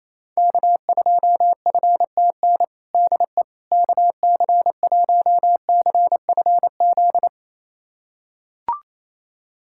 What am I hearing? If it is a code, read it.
Morse code, heard as a keyed tone, about 28 wpm, 700 Hz. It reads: K2FTN DE KC1CFZ
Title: USA Callsign DE Callsign 28wpm